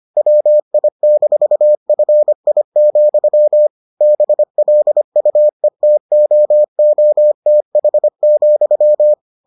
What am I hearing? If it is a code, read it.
Morse code, heard as a keyed tone, about 25 wpm, 600 Hz. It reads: WI-FI, BLUETOOTH,